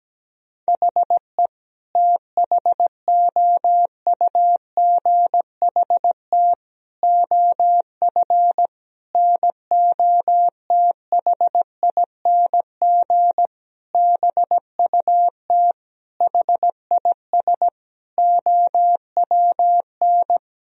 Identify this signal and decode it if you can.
Morse code, heard as a keyed tone, about 17 wpm, 700 Hz. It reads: HE THOUGHT OF NOTHING BUT HIS OWN
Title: Sentences from Top 1000 Words - Review Part 4 - 17wpm